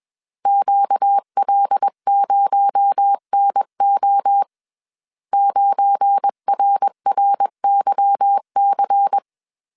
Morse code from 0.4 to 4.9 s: QL0NO